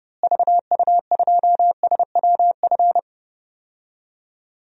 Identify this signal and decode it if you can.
Morse code, heard as a keyed tone, about 31 wpm, 700 Hz. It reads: VU2SWF
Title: DXCC-100 Call Signs - 6 Characters - Repeated 2x - 30wpm